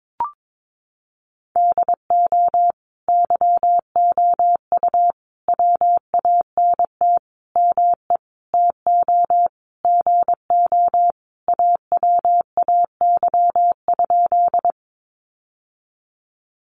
Morse code, heard as a keyed tone, about 22 wpm, 700 Hz. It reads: DO YOU WANT ME TO GO AWAY?
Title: Sentences from Top 600 Words 22wpm